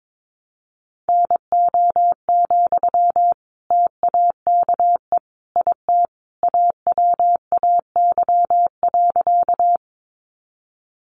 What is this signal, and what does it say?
Morse code, heard as a keyed tone, about 22 wpm, 700 Hz. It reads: NO, TAKE IT AWAY.
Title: Sentences from Top 600 Words 22wpm